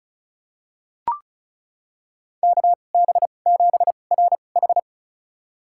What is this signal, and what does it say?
Morse code, heard as a keyed tone, about 36 wpm, spaced as for 25 wpm, 700 Hz. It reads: KB7RH
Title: USA Call Signs - 5 Characters - 2x Character Spacing - 35wpm